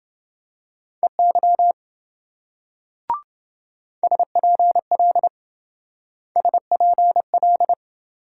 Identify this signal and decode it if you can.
Morse code, heard as a keyed tone, about 27 wpm, 700 Hz. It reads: EY SPL SPL